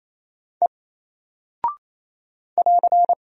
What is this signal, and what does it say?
Morse code, heard as a keyed tone, about 27 wpm, 700 Hz. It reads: E <AR>
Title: Single Character (Letter:Number:Punctuation:Prosign) Alt 28wpm